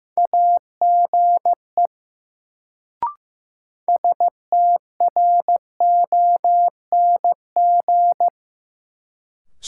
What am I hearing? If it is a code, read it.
Morse code, heard as a keyed tone, about 15 wpm, 700 Hz. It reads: AGE STRONG
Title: Sentences from Top 500 Words - New Word Drill - 15wpm